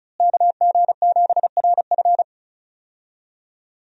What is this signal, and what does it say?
Morse code, heard as a keyed tone, about 34 wpm, 700 Hz. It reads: KG7RF
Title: USA Call Signs - 5 Characters - Repeated 3x - 35wpm